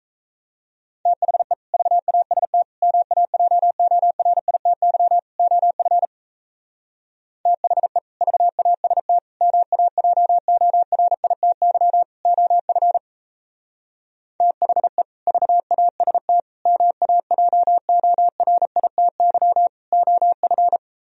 Morse code, heard as a keyed tone, about 33 wpm, 700 Hz. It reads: THE VAST MAJORITY OF THE VAST MAJORITY OF THE VAST MAJORITY OF
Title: Speed-Racing - Sets of 4 Words 28wpm